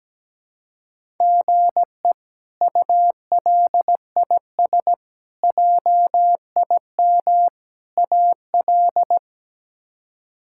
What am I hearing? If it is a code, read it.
Morse code, heard as a keyed tone, about 17 wpm, 700 Hz. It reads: GE ULIS JIM AL
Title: K1USN SST QSO Practice - 17wpm